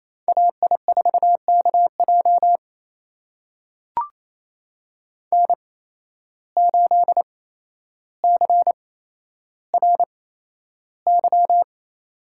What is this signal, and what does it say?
Morse code, heard as a keyed tone, about 29 wpm, spaced as for 12 wpm, 700 Hz. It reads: AI4KJ N8CRY
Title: USA Call Signs - 5 Characters - 8x Character Spacing - 28wpm